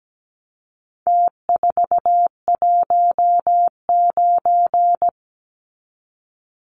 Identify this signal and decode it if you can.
Morse code, heard as a keyed tone, about 17 wpm, 700 Hz. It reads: T419
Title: Numbers - Contest Style - 4-Digit 17wpm